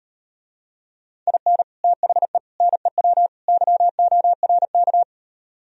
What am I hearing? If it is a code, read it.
Morse code, heard as a keyed tone, about 38 wpm, 700 Hz. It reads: IN THE NEW YORK